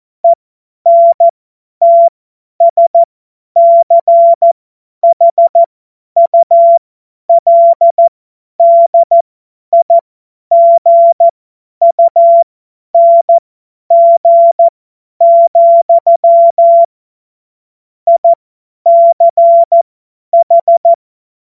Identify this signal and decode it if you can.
Morse code, heard as a keyed tone, about 14 wpm, spaced as for 10 wpm, 680 Hz. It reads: ENTSCHULDIGUNG, ICH
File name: Morsekurs / Witze / Witz 07. Dezember 2021, 50 F70